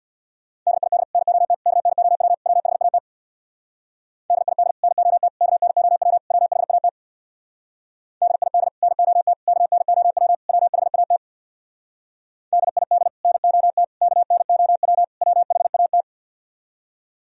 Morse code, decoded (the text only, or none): DID NOT KNOW WHAT DID NOT KNOW WHAT DID NOT KNOW WHAT DID NOT KNOW WHAT